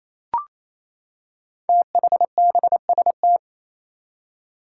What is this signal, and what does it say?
Morse code, heard as a keyed tone, about 27 wpm, 700 Hz. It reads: THBST
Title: Code Groups - 5 Letters - 28wpm